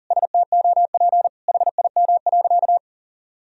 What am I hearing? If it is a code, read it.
Morse code, heard as a keyed tone, about 40 wpm, 700 Hz. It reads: STOP HIM.
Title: Sentences from Top 1000 Words - Review Part 4 - 40wpm